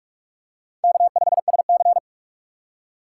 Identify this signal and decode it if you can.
Morse code, heard as a keyed tone, about 44 wpm, 690 Hz. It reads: K5SC
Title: USA Call Signs - 4 Characters - 4x Character Spacing - 45wpm